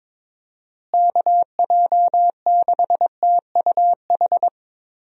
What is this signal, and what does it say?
Morse code, heard as a keyed tone, about 22 wpm, 700 Hz. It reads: KJ6TUH
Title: USA Call Signs - 6 Characters - Repeated 3x - 22wpm